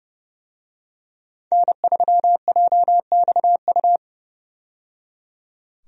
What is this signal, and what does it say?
Morse code, heard as a keyed tone, about 30 wpm, 700 Hz. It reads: N3JXU
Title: USA Call Signs - 5 Characters - 30wpm